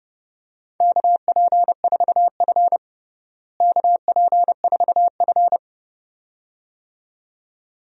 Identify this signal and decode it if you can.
Morse code, heard as a keyed tone, about 30 wpm, 700 Hz. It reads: KP4F KP4F
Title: DXCC-100 Call Signs - 4 Characters - Repeated 2x - 30wpm